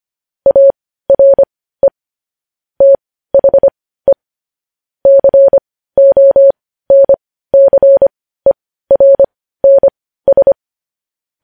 Morse code, heard as a keyed tone, about 25 wpm, spaced as for 15 wpm, 550 Hz. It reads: ARE THE CONCERNS
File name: podcast / News headlines-1x-25x15